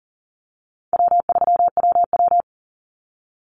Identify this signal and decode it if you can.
Morse code, heard as a keyed tone, about 40 wpm, 700 Hz. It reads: W3WW
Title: Easy Call Signs Repeated 2x - 40wpm